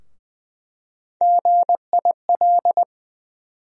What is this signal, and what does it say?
Morse code, heard as a keyed tone, about 20 wpm, 700 Hz. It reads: GIL
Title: USA Call Sign Postfixes - 3 Letters 20wpm